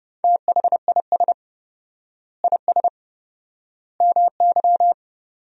Morse code, heard as a keyed tone, about 30 wpm, 700 Hz. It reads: THIS IS MY
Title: 4x Word-Spacing - Sentences from Top 300 Words - 30wpm